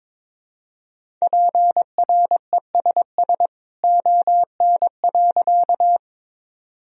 Morse code, heard as a keyed tone, about 22 wpm, 700 Hz. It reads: PRESS ON.
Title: Sentences from Top 1000 Words - Review Part 1 - 22wpm